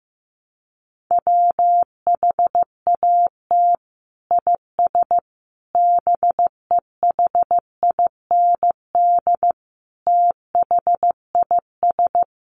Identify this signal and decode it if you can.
Morse code, heard as a keyed tone, about 15 wpm, 700 Hz. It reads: WHAT IS BEHIND THIS
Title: Sentences from Top 1100 Words 15wpm